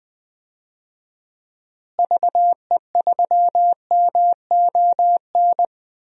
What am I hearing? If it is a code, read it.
Morse code, heard as a keyed tone, about 21 wpm, 700 Hz. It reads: VE3MON
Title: DXCC-100 Call Signs - 6 Characters - Repeated 3x - 20wpm